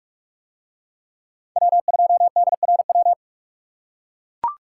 Morse code, heard as a keyed tone, about 41 wpm, 700 Hz. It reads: W2DRW